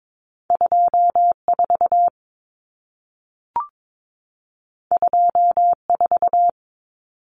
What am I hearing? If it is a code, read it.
Morse code, heard as a keyed tone, about 22 wpm, 700 Hz. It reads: 24 24
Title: Numbers - 2-Digit 22wpm